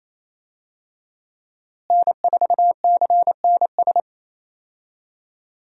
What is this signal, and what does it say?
Morse code, heard as a keyed tone, about 28 wpm, 700 Hz. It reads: N4CNS
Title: USA Call Signs - 5 Characters - Repeated 2x - 28wpm